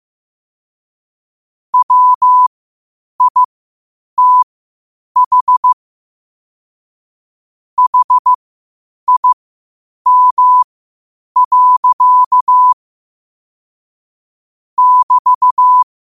Morse code, heard as a keyed tone, about 15 wpm, spaced as for 8 wpm, 1000 Hz. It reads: WITH HIM. <BT>
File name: qotdfiles / 8wpm